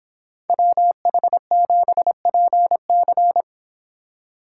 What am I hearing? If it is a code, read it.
Morse code, heard as a keyed tone, about 27 wpm, 700 Hz. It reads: WH7PC
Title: Speed Racing - USA 2x2 Call Signs 22wpm